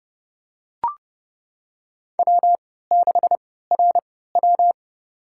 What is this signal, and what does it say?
Morse code, heard as a keyed tone, about 30 wpm, spaced as for 17 wpm, 700 Hz. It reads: W6RW
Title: USA Call Signs - 4 Characters - 3x Character Spacing - 30wpm